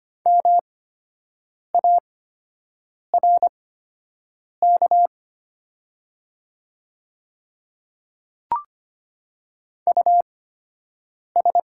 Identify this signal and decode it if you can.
Morse code, heard as a keyed tone, about 25 wpm, spaced as for 7 wpm, 700 Hz. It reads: MARK US
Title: 8x Character Spacing - Top 300 Words - 25wpm